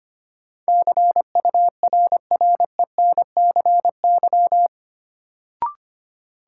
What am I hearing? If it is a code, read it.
Morse code, heard as a keyed tone, about 25 wpm, 700 Hz. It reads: CURRENCY
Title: Sentences from Top 1500 Words - New Word Drill - 25wpm